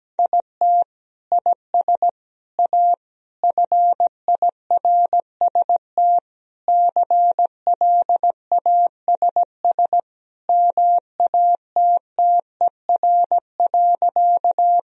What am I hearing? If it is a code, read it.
Morse code, heard as a keyed tone, about 17 wpm, 700 Hz. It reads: IT IS A FIRST CLASS MATTER.
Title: Sentences from Top 1000 Words - Review Part 2 - 17wpm